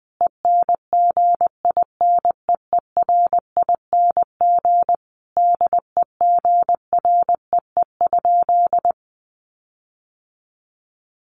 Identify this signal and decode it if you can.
Morse code, heard as a keyed tone, about 20 wpm, 700 Hz. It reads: ENGINEERING DEGREE?
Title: Sentences from Top 1500 Words 20wpm